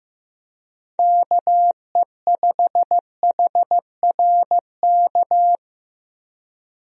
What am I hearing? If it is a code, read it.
Morse code, heard as a keyed tone, about 16 wpm, 700 Hz. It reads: KE5HRK
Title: USA Call Signs - 6 Characters - Repeated 2x - 15wpm